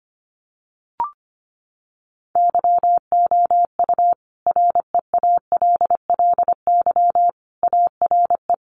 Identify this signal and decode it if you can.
Morse code, heard as a keyed tone, about 25 wpm, 700 Hz. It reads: YOU REALLY ARE
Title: Sentences from Top 1200 Words 25wpm